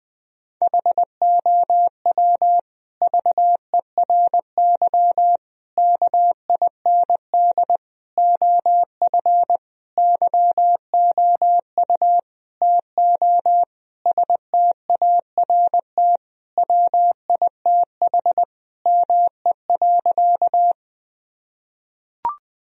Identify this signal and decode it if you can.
Morse code, heard as a keyed tone, about 20 wpm, 700 Hz. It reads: HOW VERY KIND OF YOU TO START WITH ME.
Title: Sentences from Top 1000 Words - Review Part 1 - 20wpm